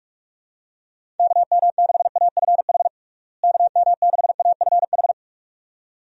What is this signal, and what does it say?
Morse code, heard as a keyed tone, about 46 wpm, 700 Hz. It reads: KM6AFH KM6AFH
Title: USA Call Signs - 6 Characters - Repeated 2x - 45wpm